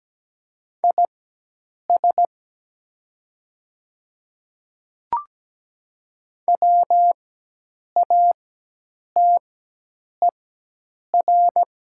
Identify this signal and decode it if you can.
Morse code, heard as a keyed tone, about 17 wpm, spaced as for 8 wpm, 700 Hz. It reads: IS WATER
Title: 4x Character Spacing - Top 300 Words - 17wpm